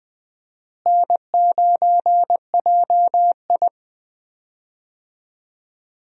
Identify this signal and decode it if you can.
Morse code, heard as a keyed tone, about 20 wpm, 690 Hz. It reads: N9JI
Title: USA Call Signs - 4 Characters - Repeated 3x - 20wpm